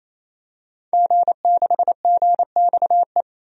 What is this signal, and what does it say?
Morse code, heard as a keyed tone, about 27 wpm, 700 Hz. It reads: G6GXE
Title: DXCC-100 Call Signs - 5 Characters - 28wpm